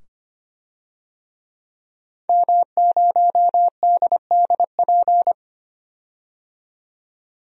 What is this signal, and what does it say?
Morse code, heard as a keyed tone, about 25 wpm, 700 Hz. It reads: M0DDP
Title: DXCC-100 Call Signs - 5 Characters - 25wpm